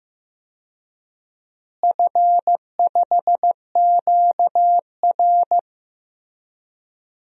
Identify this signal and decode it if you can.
Morse code, heard as a keyed tone, about 15 wpm, 700 Hz. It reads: F5QR
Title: DXCC-100 Call Signs - 4 Characters - Repeated 2x - 15wpm